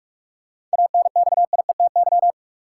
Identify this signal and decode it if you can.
Morse code, heard as a keyed tone, about 46 wpm, 700 Hz. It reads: ANXIETY
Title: Sentences from Top 1600 Words - New Word Drill Alt - 45wpm